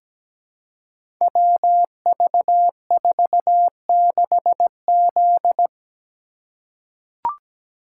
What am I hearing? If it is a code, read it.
Morse code, heard as a keyed tone, about 17 wpm, 700 Hz. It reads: WV46Z
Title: Code Groups - 5 Letter:Numbers - 17wpm